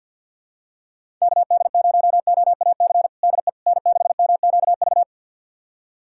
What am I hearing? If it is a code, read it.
Morse code, heard as a keyed tone, about 48 wpm, 700 Hz. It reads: KD0OAC DE N6MQV
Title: USA Callsign DE Callsign 50wpm